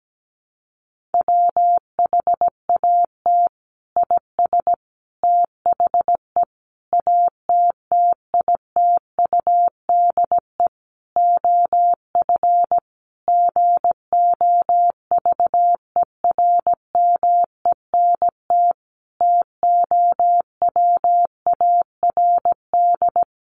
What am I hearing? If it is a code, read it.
Morse code, heard as a keyed tone, about 17 wpm, 700 Hz. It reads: WHAT IS THE ATTITUDE OF GOVERMENT TOWARD
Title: Sentences from Top 1300 Words 17wpm